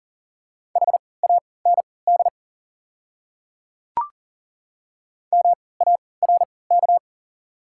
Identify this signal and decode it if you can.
Morse code, heard as a keyed tone, about 40 wpm, spaced as for 23 wpm, 690 Hz. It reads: HAND MARK
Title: 3x Character Spacing - Top 300 Words - 40wpm